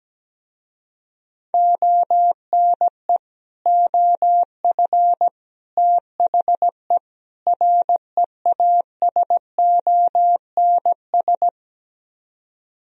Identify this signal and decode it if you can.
Morse code, heard as a keyed tone, about 17 wpm, 700 Hz. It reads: ONE OF THE REASONS
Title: Speed-Racing - Sets of 4 Words 17wpm